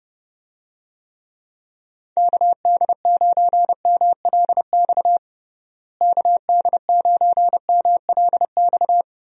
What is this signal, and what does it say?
Morse code, heard as a keyed tone, about 30 wpm, 700 Hz. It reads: KD9MLX KD9MLX
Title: USA Call Signs - 6 Characters - Repeated 3x - 30wpm